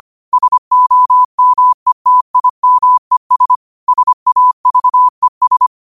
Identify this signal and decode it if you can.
Morse code, heard as a keyed tone, about 25 wpm, 1000 Hz. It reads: SOMETIMES SAVES
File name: qotdfiles / 25wpm